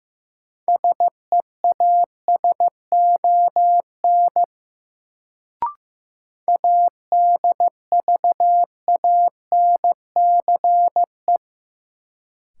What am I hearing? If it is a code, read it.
Morse code, heard as a keyed tone, about 15 wpm, 700 Hz. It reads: SEASON ADVANCE
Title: Sentences from Top 1400 Words - New Word Drill - 15wpm